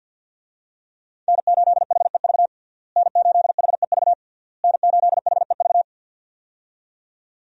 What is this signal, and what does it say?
Morse code, heard as a keyed tone, about 47 wpm, 700 Hz. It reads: N8HEV N8HEV N8HEV
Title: USA Call Signs - 5 Characters - Repeated 3x - 50wpm